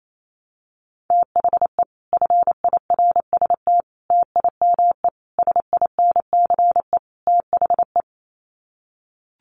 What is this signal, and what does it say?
Morse code, heard as a keyed tone, about 28 wpm, 700 Hz. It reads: THE FIRST TIME SINCE THE
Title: Sets of 5 Words Encore 28wpm